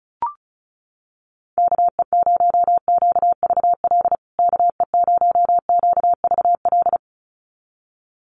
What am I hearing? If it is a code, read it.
Morse code, heard as a keyed tone, about 34 wpm, 700 Hz. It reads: KE0QVL KE0QVL
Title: USA Call Signs Repeated 2x 35wpm